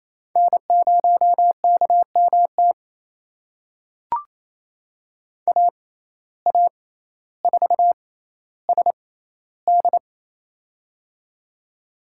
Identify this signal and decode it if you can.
Morse code, heard as a keyed tone, about 28 wpm, spaced as for 14 wpm, 700 Hz. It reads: N0KMT AA4SD
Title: USA Call Signs - 5 Characters - 6x Character Spacing - 28wpm